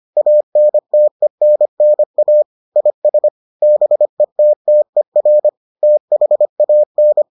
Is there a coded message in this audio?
Yes: Morse code from 0.2 to 7.2 s, heard as a keyed tone, about 25 wpm, 600 Hz. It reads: ANTENNA IS BETTER THAN